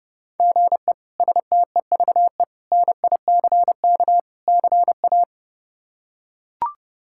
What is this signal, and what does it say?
Morse code, heard as a keyed tone, about 29 wpm, 700 Hz. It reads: GE STEVE NICK CA
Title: K1USN SST QSO Practice - 30wpm